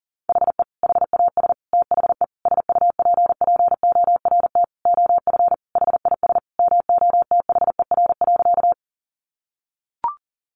Morse code, heard as a keyed tone, about 40 wpm, 690 Hz. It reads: HE HAS THE SUPPORT OF HIS MOTHER.
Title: Sentences from Top 800 Words 40wpm